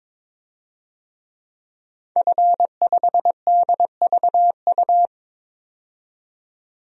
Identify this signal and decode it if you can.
Morse code, heard as a keyed tone, about 22 wpm, 700 Hz. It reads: F5DVU
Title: DXCC-100 Call Signs - 5 Characters - 22wpm